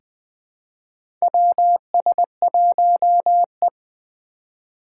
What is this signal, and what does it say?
Morse code, heard as a keyed tone, about 20 wpm, 700 Hz. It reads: WS1E
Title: USA Call Signs - 4x Character Spacing - 20wpm